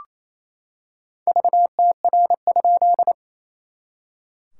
Morse code, heard as a keyed tone, about 28 wpm, 700 Hz. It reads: VTR?
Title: SOTA QSO Practice 28wpm